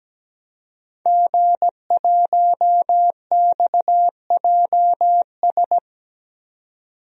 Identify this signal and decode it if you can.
Morse code, heard as a keyed tone, about 17 wpm, 700 Hz. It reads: G1XJS
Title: DXCC-100 Call Signs - 5 Characters - 17wpm